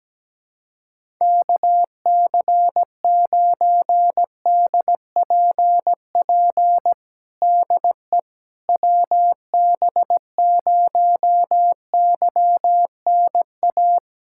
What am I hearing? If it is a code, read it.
Morse code, heard as a keyed tone, about 17 wpm, 700 Hz. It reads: KC9DPP DE WB0YNA
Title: USA Callsign DE Callsign - 6 Characters - 17wpm